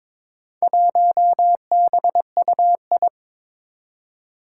Morse code, heard as a keyed tone, about 22 wpm, 700 Hz. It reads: 1BUI
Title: Four Challenging Letter-Number Combinations 22wpm